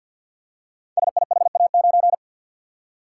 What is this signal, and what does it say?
Morse code, heard as a keyed tone, about 48 wpm, 700 Hz. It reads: SIHA9
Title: Code Groups - 5 Letter:Numbers - 50wpm